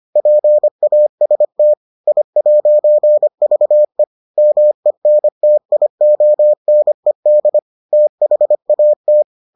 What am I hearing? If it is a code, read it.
Morse code, heard as a keyed tone, about 25 wpm, 600 Hz. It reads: PAST I'VE MENTIONED THAT